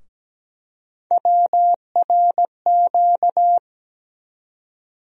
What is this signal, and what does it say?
Morse code, heard as a keyed tone, about 17 wpm, 700 Hz. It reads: WRQ
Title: USA Call Sign Postfixes - 3 Letters 17wpm